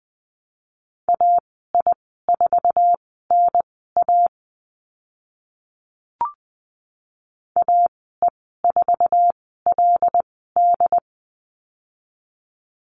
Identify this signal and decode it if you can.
Morse code, heard as a keyed tone, about 20 wpm, spaced as for 14 wpm, 700 Hz. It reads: AI4NA AE4LD
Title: 076 Lesson 13g - Letter L - Call Signs 20wpm